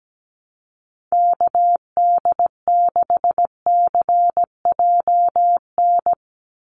Morse code, heard as a keyed tone, about 17 wpm, 700 Hz. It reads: KD6CJN
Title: USA Call Signs 17wpm